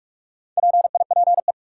Morse code, heard as a keyed tone, about 43 wpm, 700 Hz. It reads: PIPE
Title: Sentences from Top 1500 Words - New Word Drill - 45wpm